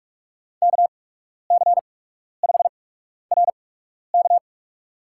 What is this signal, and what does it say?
Morse code, heard as a keyed tone, about 46 wpm, spaced as for 12 wpm, 700 Hz. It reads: KC5RK
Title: USA Call Signs - 5 Characters - 8x Character Spacing - 45wpm